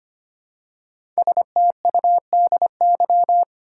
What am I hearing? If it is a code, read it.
Morse code, heard as a keyed tone, about 25 wpm, 700 Hz. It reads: STUDY
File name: 5 Letter Words - First 3 Letters Given - 25wpm